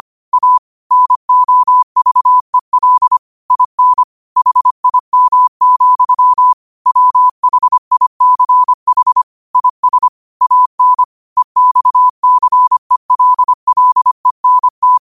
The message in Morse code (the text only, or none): A NOVEL IN HIM, WHICH IS AN EXCELLENT